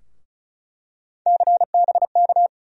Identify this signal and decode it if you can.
Morse code, heard as a keyed tone, about 36 wpm, 700 Hz. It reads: CBK
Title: USA Call Sign Postfixes - 3 Letters 35wpm